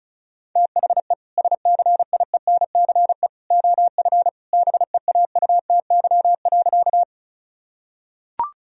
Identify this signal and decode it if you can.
Morse code, heard as a keyed tone, about 36 wpm, 700 Hz. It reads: THE SCIENCE OF BEAUTY.
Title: Sentences from Top 1000 Words - Review Part 2 - 35wpm